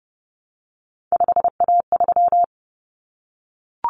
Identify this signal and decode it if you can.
Morse code, heard as a keyed tone, about 30 wpm, 700 Hz. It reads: 5A3
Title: Common Cut Numbers - 3-Digit 30wpm